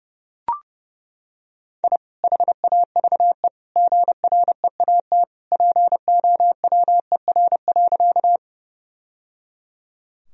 Morse code, heard as a keyed tone, about 30 wpm, 700 Hz. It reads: I HAVE GREAT POWER.
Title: Sentences from Top 1000 Words - Review Part 2 - 30wpm